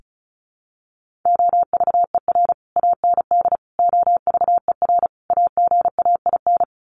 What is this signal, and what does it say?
Morse code, heard as a keyed tone, about 34 wpm, 700 Hz. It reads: OVER AND OVER AGAIN
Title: Sets of 4 Words 35wpm